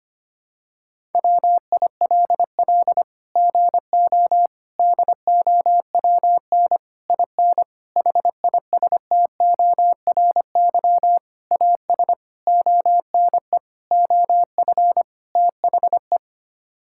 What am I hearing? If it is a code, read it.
Morse code, heard as a keyed tone, about 25 wpm, 700 Hz. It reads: WILL GO DOWN IN HISTORY AS ONE OF THE
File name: Sets of 9 Words - 25wpm